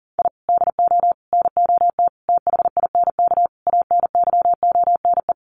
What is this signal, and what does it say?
Morse code, heard as a keyed tone, about 40 wpm, 700 Hz. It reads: I DO NOT THINK ANYONE
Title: Sets of 5 Words Encore 40wpm